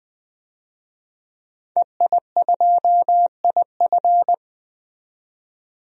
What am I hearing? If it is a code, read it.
Morse code, heard as a keyed tone, about 21 wpm, 700 Hz. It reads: EI2IF
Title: DXCC-100 Call Signs - 5 Characters - 20wpm